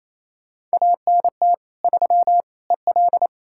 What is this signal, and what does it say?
Morse code, heard as a keyed tone, about 37 wpm, 700 Hz. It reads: ANT 3 EL